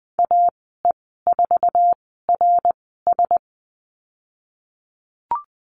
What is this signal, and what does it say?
Morse code, heard as a keyed tone, about 20 wpm, spaced as for 15 wpm, 700 Hz. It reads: AE4RS
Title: 069 Lesson 12g - Letter d - Call Signs 20wpm